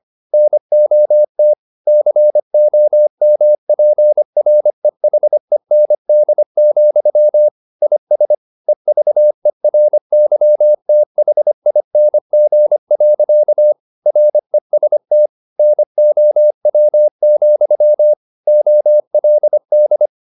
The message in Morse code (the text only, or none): NOT COMPREHEND, IS EVERYTHING. REST NOW, OLD